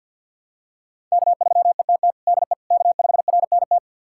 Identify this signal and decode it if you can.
Morse code, heard as a keyed tone, about 53 wpm, 700 Hz. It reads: K3ETT DE K5RNT
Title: USA Callsign DE Callsign - 5 Characters - 50wpm